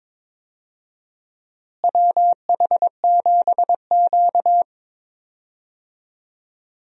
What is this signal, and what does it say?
Morse code, heard as a keyed tone, about 22 wpm, 700 Hz. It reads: WH7Q
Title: USA Call Signs - 4 Characters - Repeated 3x - 22wpm